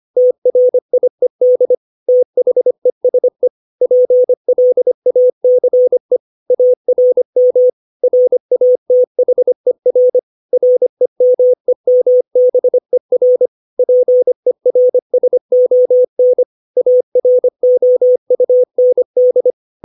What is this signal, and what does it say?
Morse code, heard as a keyed tone, about 25 wpm, 500 Hz. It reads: TRIED THESE PLACE ARM RATHER REMEMBER PERSON AROUND
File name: 632 words / 632 Words 25WPM